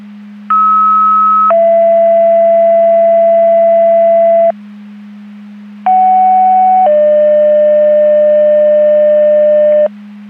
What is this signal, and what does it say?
Two-tone page, heard as a one-second tone, then a three-second tone, about 1290 then 670 Hz, about 750 then 600 Hz. It is heard from 0.5 to 9.9 s.
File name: Portage County 154.3850 Fire EMS Paging